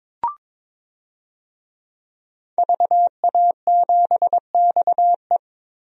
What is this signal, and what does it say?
Morse code, heard as a keyed tone, about 22 wpm, 700 Hz. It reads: VA7XE
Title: DXCC-100 Call Signs - 5 Characters - Repeated 3x - 22wpm